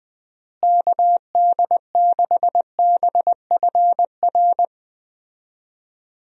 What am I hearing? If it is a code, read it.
Morse code, heard as a keyed tone, about 21 wpm, 700 Hz. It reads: KD6BFR
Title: USA Call Signs - 6 Characters - Repeated 2x - 20wpm